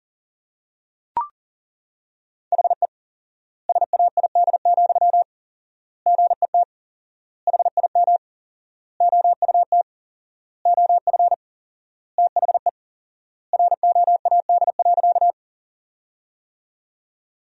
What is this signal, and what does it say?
Morse code, heard as a keyed tone, about 37 wpm, 700 Hz. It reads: HE SAID, GET HIM OUT OF THE ROAD.
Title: 4x Word-Spacing - Sentences from Top 400 Words - 40wpm